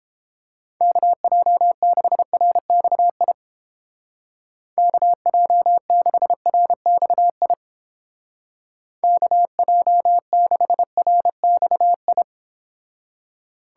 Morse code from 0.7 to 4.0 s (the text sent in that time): KJ6RXI